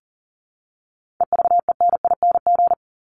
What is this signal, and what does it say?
Morse code, heard as a keyed tone, about 40 wpm, 700 Hz. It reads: EVENING
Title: Top 1501-1750 Words 40wpm